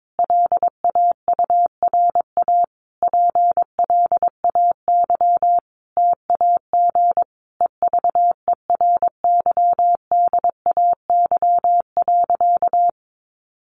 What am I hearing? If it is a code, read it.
Morse code, heard as a keyed tone, about 22 wpm, 700 Hz. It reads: LAURA PLAY TAG EVERYDAY.
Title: Additional Easy Phrases 22wpm